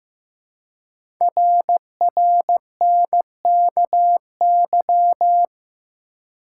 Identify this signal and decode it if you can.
Morse code, heard as a keyed tone, about 15 wpm, 700 Hz. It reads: RRNKY
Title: Code Groups - 5 Letters - 15wpm